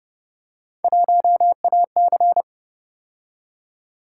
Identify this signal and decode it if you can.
Morse code, heard as a keyed tone, about 30 wpm, 700 Hz. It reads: 1AC
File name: Speed-Racing - Three Letter-Number Combinations - 25wpm